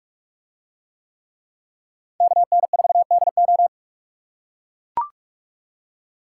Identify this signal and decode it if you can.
Morse code, heard as a keyed tone, about 39 wpm, 700 Hz. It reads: KN4DO